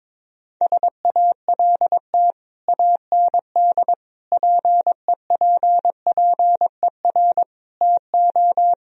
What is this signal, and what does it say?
Morse code, heard as a keyed tone, about 22 wpm, 700 Hz. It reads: SALT AND PEPPER TO
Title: Speed-Racing - Sets of 5 Words 22wpm